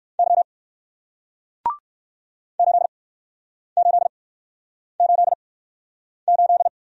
Morse code, heard as a keyed tone, about 60 wpm, spaced as for 9 wpm, 700 Hz. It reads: Y 8888